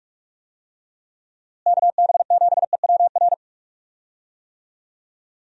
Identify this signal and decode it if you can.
Morse code, heard as a keyed tone, about 46 wpm, 700 Hz. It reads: KB7EWR
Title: USA Call Signs - 6 Characters - Repeated 2x - 45wpm